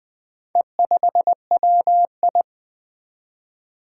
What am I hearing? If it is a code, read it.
Morse code, heard as a keyed tone, about 20 wpm, 700 Hz. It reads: E5WI
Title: Four Challenging Letter-Number Combinations 20wpm